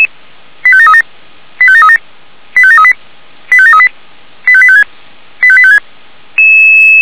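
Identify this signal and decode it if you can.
Five-tone selective call (ZVEI), heard as quick runs of five tones, tones of about 70 ms. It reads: 85737 85737 85738 85738 85856 85856
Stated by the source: ZVEI five-tone selective call received on SDR.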